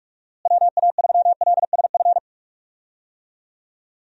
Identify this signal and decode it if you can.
Morse code, heard as a keyed tone, about 46 wpm, 700 Hz. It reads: WA3LSF
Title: USA Call Signs - 6 Characters - Repeated 2x - 45wpm